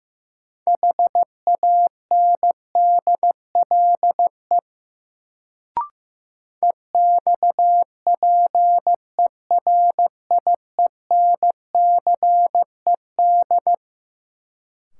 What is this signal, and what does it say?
Morse code, heard as a keyed tone, about 15 wpm, 700 Hz. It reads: HANDLE EXPERIENCED
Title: Sentences from Top 1100 Words - New Word Drill - 15wpm